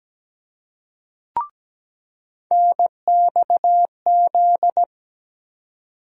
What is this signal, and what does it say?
Morse code, heard as a keyed tone, about 17 wpm, 700 Hz. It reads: NXZ
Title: USA Call Sign Postfixes - 3 Letters 17wpm